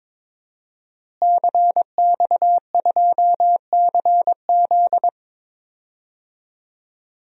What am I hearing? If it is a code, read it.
Morse code, heard as a keyed tone, about 22 wpm, 700 Hz. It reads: CX2CZ
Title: DXCC-100 Call Signs - 5 Characters - Repeated 3x - 22wpm